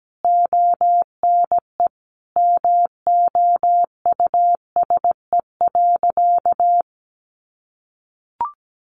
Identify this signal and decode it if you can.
Morse code, heard as a keyed tone, about 17 wpm, 700 Hz. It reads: ONE MOUSE.
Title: A Sound Of Thunder 17wpm